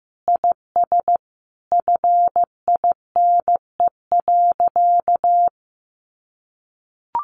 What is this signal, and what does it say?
Morse code, heard as a keyed tone, about 15 wpm, 700 Hz. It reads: IS FINE.
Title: Sentences from Top 800 Words 15wpm